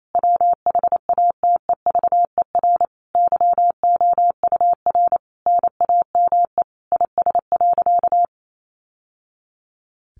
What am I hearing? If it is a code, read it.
Morse code, heard as a keyed tone, about 28 wpm, 700 Hz. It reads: WHATEVER YOUR NAME IS.
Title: Sentences from Top 1100 Words 28wpm